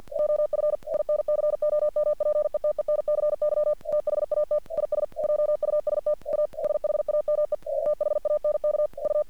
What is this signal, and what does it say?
Morse code, heard as a keyed tone, about 48 wpm, 600 Hz. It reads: OF INCOMPETENCY THAT IS JUST A SHAME E THANK U
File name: archived / CRSnet